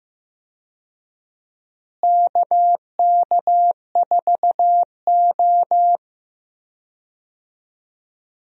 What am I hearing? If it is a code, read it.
Morse code, heard as a keyed tone, about 15 wpm, 700 Hz. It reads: KK4O
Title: USA Call Signs - 4 Characters - Repeated 2x - 15wpm